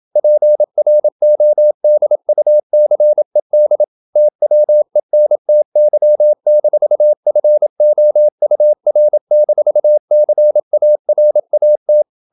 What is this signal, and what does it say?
Morse code, heard as a keyed tone, about 28 wpm, 600 Hz. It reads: PRODUCED TWENTY-FOUR-CARAT
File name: Never 0027wpm